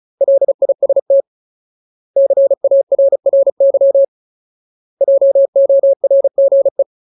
Morse code, heard as a keyed tone, about 34 wpm, 550 Hz. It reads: LIST CARRY JORGE
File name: Ultimate Word List / 35WPM Ultimate